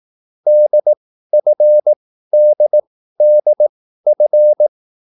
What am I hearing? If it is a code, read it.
Morse code, heard as a keyed tone, about 18 wpm, spaced as for 13 wpm, 600 Hz. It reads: DFDDF
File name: QSL / CW lesson